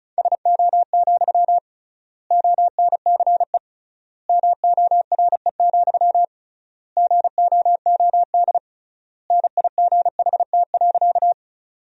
Morse code, heard as a keyed tone, about 36 wpm, 700 Hz. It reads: SO, ONCE MORE, GOOD NIGHT.
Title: 3x Word-Spacing - Sentences from Top 300 Words - 35wpm